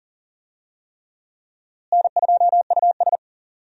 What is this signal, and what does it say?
Morse code, heard as a keyed tone, about 40 wpm, 700 Hz. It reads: N2US
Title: USA Call Signs - 4 Characters - Repeated 3x - 40wpm